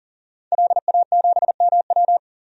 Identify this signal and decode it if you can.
Morse code, heard as a keyed tone, about 40 wpm, 700 Hz. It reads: LA7MW
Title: DXCC-100 Call Signs - 5 Characters - 40wpm